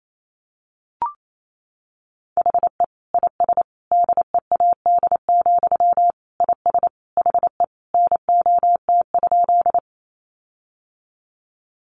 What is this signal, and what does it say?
Morse code, heard as a keyed tone, about 28 wpm, 700 Hz. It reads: HE IS DEAD, IS HE NOT?
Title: Sentences from Top 1500 Words 28wpm